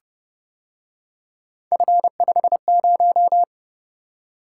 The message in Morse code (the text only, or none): F50